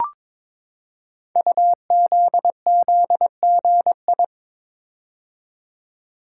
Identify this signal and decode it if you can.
Morse code, heard as a keyed tone, about 22 wpm, 700 Hz. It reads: UZZGI
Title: Code Groups - 5 Letters - 22wpm